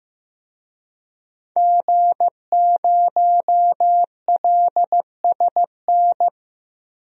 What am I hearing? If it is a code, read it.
Morse code, heard as a keyed tone, about 16 wpm, 700 Hz. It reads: G0LSN
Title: DXCC-100 Call Signs - 5 Characters - 15wpm